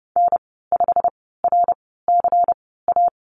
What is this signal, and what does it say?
Morse code, heard as a keyed tone, about 30 wpm, spaced as for 17 wpm, 700 Hz. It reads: N5RCA
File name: Lesson 19g - Letter w - Call Signs 30wpm